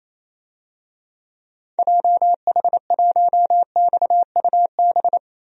Morse code, heard as a keyed tone, about 28 wpm, 700 Hz. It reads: JH1XUB
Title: DXCC-100 Call Signs - 6 Characters - 28wpm